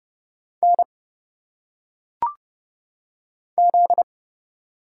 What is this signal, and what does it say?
Morse code, heard as a keyed tone, about 30 wpm, 700 Hz. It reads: N Z
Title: Single Letters - Phonetic Pronunciation 30wpm